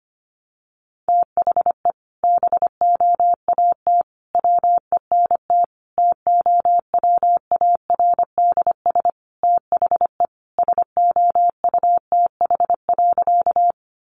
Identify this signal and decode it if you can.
Morse code, heard as a keyed tone, about 25 wpm, 700 Hz. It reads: THE BOAT WENT TOWARDS THE SOUTH.
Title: Sentences from Top 1100 Words 25wpm